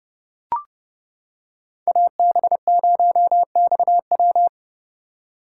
Morse code, heard as a keyed tone, about 29 wpm, 700 Hz. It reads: AB0XW